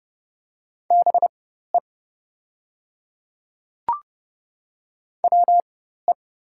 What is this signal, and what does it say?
Morse code, heard as a keyed tone, about 29 wpm, spaced as for 14 wpm, 690 Hz. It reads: BE WE
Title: 4x Character Spacing - Top 100 Words - 30wpm